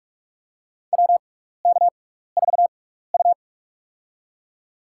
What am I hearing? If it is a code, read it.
Morse code, heard as a keyed tone, about 46 wpm, spaced as for 15 wpm, 700 Hz. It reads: WK4U
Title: USA Call Signs - 6x Character Spacing - 45wpm